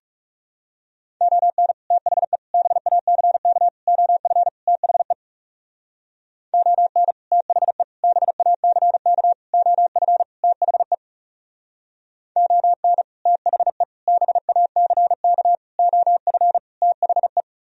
Morse code, heard as a keyed tone, about 37 wpm, 700 Hz. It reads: ON THE BACK OF THE ON THE BACK OF THE ON THE BACK OF THE
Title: Speed-Racing - Sets of 5 Words 30wpm